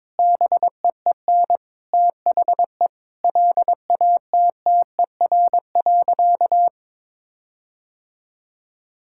Morse code, heard as a keyed tone, about 22 wpm, 700 Hz. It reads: BEEN THE LATTER.
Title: Sentences from Top 1600 Words 22wpm